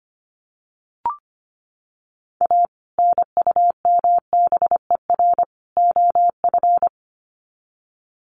Morse code, heard as a keyed tone, about 25 wpm, 700 Hz. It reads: A NUMBER OF
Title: Sets of 3 Words 25wpm